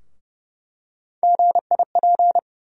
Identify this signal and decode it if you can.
Morse code, heard as a keyed tone, about 30 wpm, 700 Hz. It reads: GIP
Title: USA Call Sign Postfixes - 3 Letters 30wpm